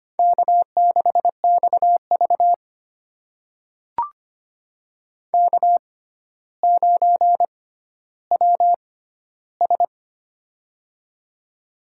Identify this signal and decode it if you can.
Morse code, heard as a keyed tone, about 25 wpm, spaced as for 13 wpm, 700 Hz. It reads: K6XV K9WS
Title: USA Call Signs - 4 Characters - 6x Character Spacing - 25wpm